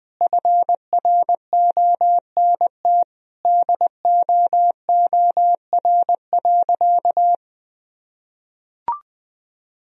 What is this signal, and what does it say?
Morse code, heard as a keyed tone, about 20 wpm, 700 Hz. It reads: FRONT DOOR.
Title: Sentences from Top 1000 Words - Review Part 2 - 20wpm